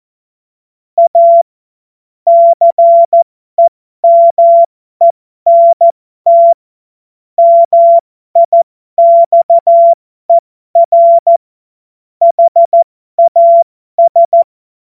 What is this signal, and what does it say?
Morse code, heard as a keyed tone, about 14 wpm, spaced as for 12 wpm, 680 Hz. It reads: A CEMENT MIXER HAS
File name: Morsekurs / Witze